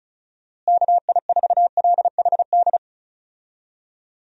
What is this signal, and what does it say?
Morse code, heard as a keyed tone, about 36 wpm, 700 Hz. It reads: KI4LHD
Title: USA Call Signs - 6 Characters - Repeated 2x - 35wpm